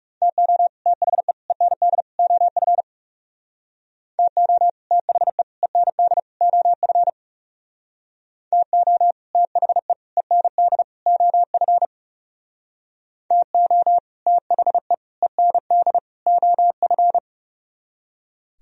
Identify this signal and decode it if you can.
Morse code, heard as a keyed tone, about 40 wpm, 690 Hz. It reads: TO THE END OF TO THE END OF TO THE END OF TO THE END OF